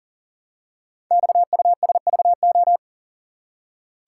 Morse code, heard as a keyed tone, about 42 wpm, 700 Hz. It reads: XUSVO
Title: Code Groups - 5 Letter:Numbers - 40wpm